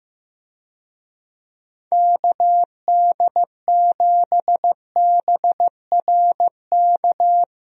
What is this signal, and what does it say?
Morse code, heard as a keyed tone, about 16 wpm, 700 Hz. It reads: KD7BRK
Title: USA Call Signs - 6 Characters - Repeated 3x - 15wpm